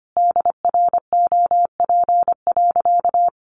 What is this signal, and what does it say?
Morse code, heard as a keyed tone, about 25 wpm, 700 Hz. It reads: DROP.
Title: Sentences from Top 1400 Words 25wpm